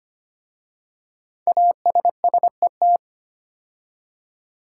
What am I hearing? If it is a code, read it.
Morse code, heard as a keyed tone, about 25 wpm, 700 Hz. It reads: ASSET
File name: 5 Letter Words - First 3 Letters Given - 25wpm